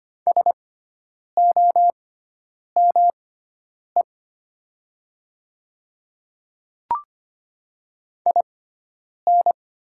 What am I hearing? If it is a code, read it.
Morse code, heard as a keyed tone, about 25 wpm, spaced as for 9 wpm, 700 Hz. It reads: SOME IN
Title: 6x Character Spacing - Top 100 Words - 25wpm